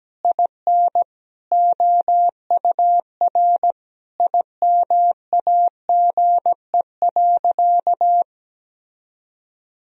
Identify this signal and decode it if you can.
Morse code, heard as a keyed tone, about 17 wpm, 700 Hz. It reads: IN OUR IMAGE.
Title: Sentences from Top 1000 Words - Review Part 4 - 17wpm